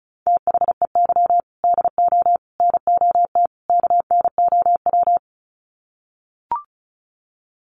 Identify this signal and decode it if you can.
Morse code, heard as a keyed tone, about 34 wpm, 700 Hz. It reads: THEY DO NOT KNOW
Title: Sets of 4 Words 35wpm